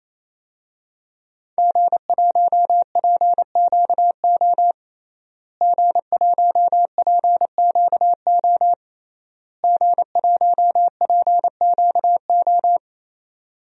Morse code, heard as a keyed tone, about 28 wpm, 700 Hz. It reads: G1PQO G1PQO G1PQO
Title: DXCC-100 Call Signs - 5 Characters - Repeated 3x - 28wpm